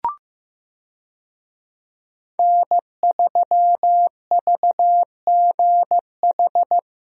Morse code, heard as a keyed tone, about 16 wpm, 700 Hz. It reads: N3VGH
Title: USA Call Signs - 5 Characters - Repeated 3x - 15wpm